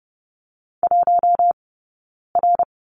Morse code, heard as a keyed tone, about 30 wpm, 700 Hz. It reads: 1 R
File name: Lesson 20d - Number 3 - Two Characters 30wpm